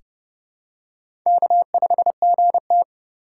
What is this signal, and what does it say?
Morse code, heard as a keyed tone, about 30 wpm, 700 Hz. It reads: K5GT
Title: USA Call Signs - 4 Characters - 4x Character Spacing - 30wpm